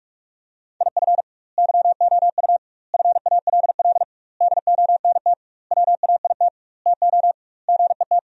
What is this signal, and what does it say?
Morse code, heard as a keyed tone, about 46 wpm, 700 Hz. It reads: IF YOU FALL DONT WAIT TO GET
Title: 2x Word-Spacing - Sentences from Top 500 Words - 45wpm